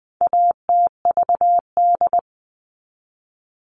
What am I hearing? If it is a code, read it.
Morse code, heard as a keyed tone, about 20 wpm, 700 Hz. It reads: ATVD
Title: All Cut Numbers - 4-Digit 20wpm